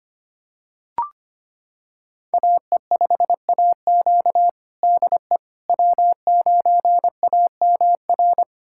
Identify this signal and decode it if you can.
Morse code, heard as a keyed tone, about 25 wpm, 700 Hz. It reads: AE5AQ DE W9AMR
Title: USA Callsign DE Callsign - 5 Characters - 25wpm